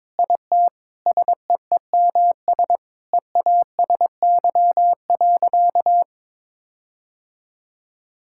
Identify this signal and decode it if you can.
Morse code, heard as a keyed tone, about 22 wpm, 700 Hz. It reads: IT SEEMS EASY.
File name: practice / 22wpm / Sentences with 4-Word Phrases - 22wpm